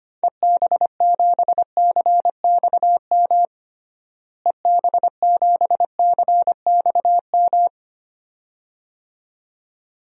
Morse code, heard as a keyed tone, about 25 wpm, 700 Hz. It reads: EB7CXM EB7CXM
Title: DXCC-100 Call Signs - 6 Characters - Repeated 2x - 25wpm